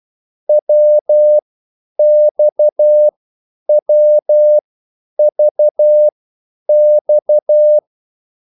Morse code, heard as a keyed tone, about 12 wpm, spaced as for 9 wpm, 600 Hz. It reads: WXWVX
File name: cwbook